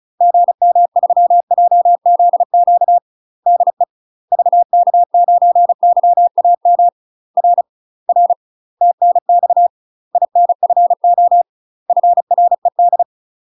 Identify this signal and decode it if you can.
Morse code, heard as a keyed tone, about 36 wpm, 700 Hz. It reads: GM3JZQ DE VK9YAM R R TNX INFO FRED
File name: UK MASTER QSOs TRAINING / 35WPM / UK MASTER QSOs 0000